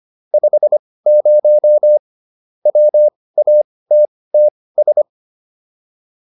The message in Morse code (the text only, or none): 50 WATTS